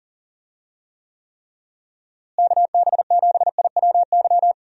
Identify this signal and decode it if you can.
Morse code, heard as a keyed tone, about 40 wpm, 700 Hz. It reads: KB7IWY
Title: USA Call Signs - 6 Characters - Repeated 3x - 40wpm